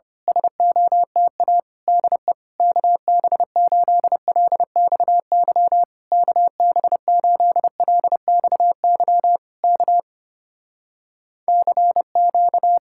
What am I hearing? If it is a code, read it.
Morse code, heard as a keyed tone, about 26 wpm, 700 Hz. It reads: SOTA DE KB8LXY KB8LXY K CQ
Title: Speed-Racing Common QSO Phrases Encore - 25wpm